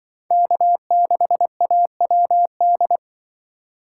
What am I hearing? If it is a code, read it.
Morse code, heard as a keyed tone, about 24 wpm, 700 Hz. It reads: K6AWD
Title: Speed Racing - USA Call Signs 20wpm